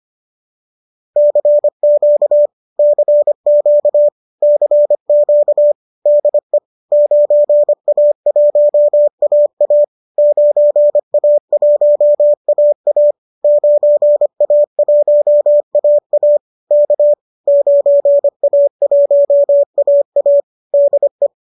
Morse code, heard as a keyed tone, about 25 wpm, 590 Hz. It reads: CQ CQ CQ DE 9A1AA 9A1AA 9A1AA K 9A1AA DE